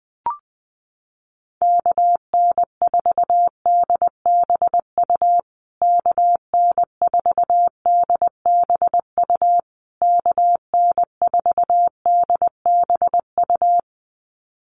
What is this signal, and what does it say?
Morse code, heard as a keyed tone, about 20 wpm, 700 Hz. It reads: KN4DBU KN4DBU KN4DBU
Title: USA Call Signs Repeated 3x 20wpm